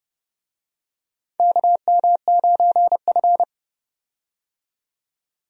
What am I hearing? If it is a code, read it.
Morse code, heard as a keyed tone, about 32 wpm, 700 Hz. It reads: KM9F
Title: USA Call Signs - 4 Characters - 30wpm